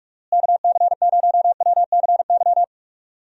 Morse code, heard as a keyed tone, about 43 wpm, 700 Hz. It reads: KC0WCY
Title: USA Call Signs - 6 Characters - Repeated 3x - 45wpm